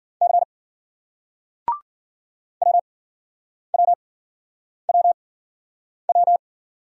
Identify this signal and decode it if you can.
Morse code, heard as a keyed tone, about 56 wpm, spaced as for 9 wpm, 700 Hz. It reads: C WWWW